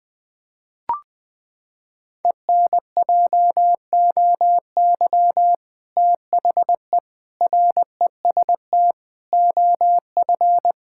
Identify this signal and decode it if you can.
Morse code, heard as a keyed tone, about 21 wpm, 700 Hz. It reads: ENJOY THE REST OF
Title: Sentences with 2-Word Phrases - 20wpm